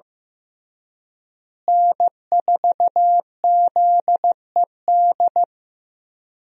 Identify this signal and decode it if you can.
Morse code, heard as a keyed tone, about 15 wpm, 700 Hz. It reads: N4ZED
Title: USA Call Signs - 5 Characters - Repeated 3x - 15wpm